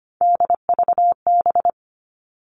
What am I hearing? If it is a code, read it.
Morse code, heard as a keyed tone, about 25 wpm, 700 Hz. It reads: DVB
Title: All Cut Numbers - 3-Digit 25wpm